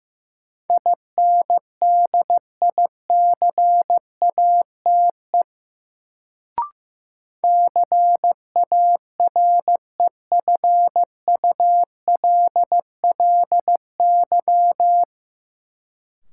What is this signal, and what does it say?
Morse code, heard as a keyed tone, about 16 wpm, 700 Hz. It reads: INDICATE CAREFULLY
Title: Sentences from Top 1300 Words - New Word Drill Alt - 15wpm